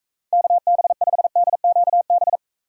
Morse code, heard as a keyed tone, about 44 wpm, 700 Hz. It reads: KB5DQB